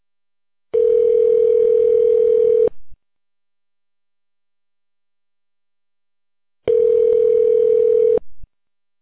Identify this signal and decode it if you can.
Ringback tone (North America), 440 + 480 Hz.